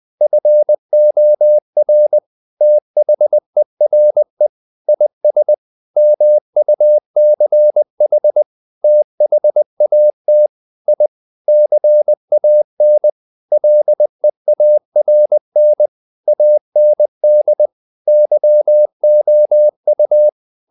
Morse code, heard as a keyed tone, about 20 wpm, 600 Hz. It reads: FOR THERE IS MUCH THAT I CAN LEARN AND YOU
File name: the old man and the sea part 2 20wpm